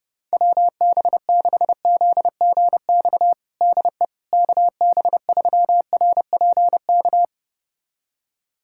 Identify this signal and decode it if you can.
Morse code, heard as a keyed tone, about 32 wpm, 700 Hz. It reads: WB6ZGX DE KB3RPK
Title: USA Callsign DE Callsign 30wpm